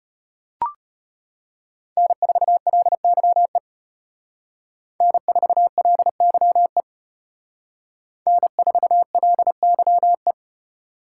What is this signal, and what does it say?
Morse code, heard as a keyed tone, about 32 wpm, 700 Hz. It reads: N4LYE N4LYE N4LYE